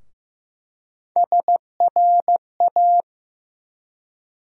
Morse code, heard as a keyed tone, about 16 wpm, 700 Hz. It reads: SRA
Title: USA Call Sign Postfixes - 3 Letters 15wpm